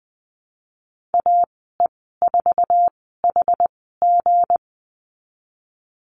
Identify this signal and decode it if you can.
Morse code, heard as a keyed tone, about 20 wpm, spaced as for 15 wpm, 700 Hz. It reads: AE4HG
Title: 167 Lesson 27g - Number 7 - Call Signs 20wpm